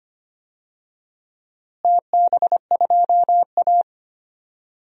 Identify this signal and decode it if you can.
Morse code, heard as a keyed tone, about 25 wpm, 700 Hz. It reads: TB2A
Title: DXCC-100 Call Signs - 4 Characters - Repeated 3x - 25wpm